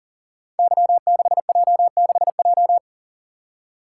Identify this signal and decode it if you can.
Morse code, heard as a keyed tone, about 40 wpm, 700 Hz. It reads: Y6J6J
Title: Code Groups - 5 Letter:Numbers - 40wpm